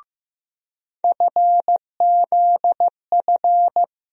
Morse code, heard as a keyed tone, about 16 wpm, 700 Hz. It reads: FZF
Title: USA Call Sign Postfixes - 3 Letters 15wpm